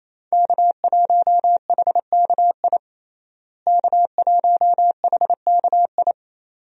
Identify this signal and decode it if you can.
Morse code, heard as a keyed tone, about 28 wpm, 700 Hz. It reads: K1HKI K1HKI
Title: USA Call Signs - 5 Characters - Repeated 2x - 28wpm